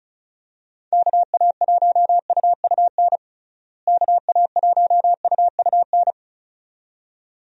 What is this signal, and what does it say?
Morse code, heard as a keyed tone, about 35 wpm, 700 Hz. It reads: KA1UUN KA1UUN
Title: USA Call Signs - 6 Characters - Repeated 2x - 35wpm